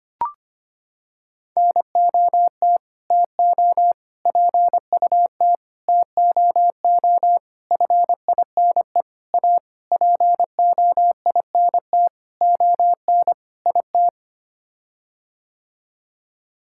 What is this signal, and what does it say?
Morse code, heard as a keyed tone, about 25 wpm, 700 Hz. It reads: NOT TO PUT TOO FINE A POINT ON IT
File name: Sets of 9 Words - 25wpm